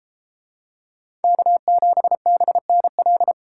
Morse code, heard as a keyed tone, about 32 wpm, 700 Hz. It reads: K7BNL